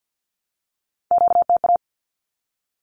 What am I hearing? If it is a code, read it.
Morse code, heard as a keyed tone, about 50 wpm, 700 Hz. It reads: QTA
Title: Q-Codes 50wpm